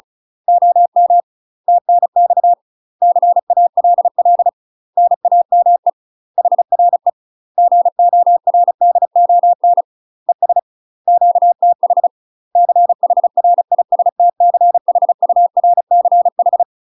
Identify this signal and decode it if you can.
Morse code, heard as a keyed tone, about 36 wpm, 700 Hz. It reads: OM TNX CALL NAME HRE GORDON ES QTH CHRISTCHURCH
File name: UK MASTER QSOs TRAINING / 35WPM / UK MASTER QSOs 0017